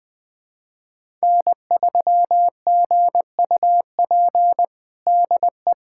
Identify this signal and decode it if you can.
Morse code, heard as a keyed tone, about 21 wpm, 700 Hz. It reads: N3GUP DE
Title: USA Callsign DE Callsign 20wpm